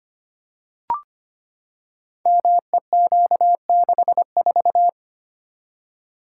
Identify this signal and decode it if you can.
Morse code, heard as a keyed tone, about 25 wpm, 700 Hz. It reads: MEQ64
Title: Code Groups - 5 Letter:Numbers - 25wpm